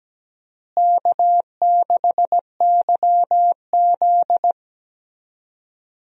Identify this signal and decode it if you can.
Morse code, heard as a keyed tone, about 17 wpm, 700 Hz. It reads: K6YZ
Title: USA Call Signs - 4 Characters - Repeated 3x - 17wpm